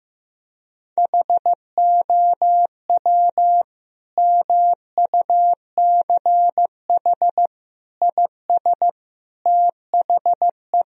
Morse code, heard as a keyed tone, about 15 wpm, 700 Hz. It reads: HOW MUCH IS THE
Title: Sentences from Top 1000 Words - Review Part 1 - 15wpm